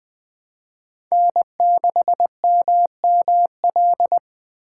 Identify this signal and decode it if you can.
Morse code, heard as a keyed tone, about 20 wpm, 700 Hz. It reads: N6MML
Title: USA Call Signs - 5 Characters - 3x Character Spacing - 20wpm